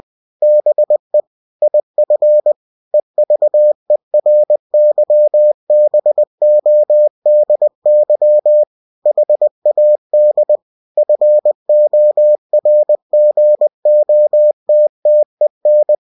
Morse code, heard as a keyed tone, about 20 wpm, 600 Hz. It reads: BE IF EVERYBODY HAD FORGOTTEN
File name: cw-books / winnie-the-pooh / winnie-20wpm-page-0103